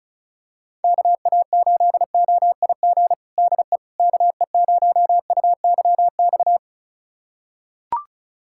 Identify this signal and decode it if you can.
Morse code, heard as a keyed tone, about 36 wpm, 700 Hz. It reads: KA8OIG DE KE0UYX
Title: USA Callsign DE Callsign - 6 Characters - 35wpm